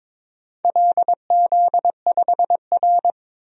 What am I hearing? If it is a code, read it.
Morse code, heard as a keyed tone, about 23 wpm, 700 Hz. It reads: LZ5R
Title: DXCC-100 Call Signs - 4 Characters - 22wpm